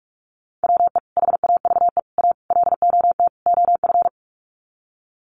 Morse code, heard as a keyed tone, about 44 wpm, 700 Hz. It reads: WE HAVE A LOT OF
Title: Sets of 5 Words 45wpm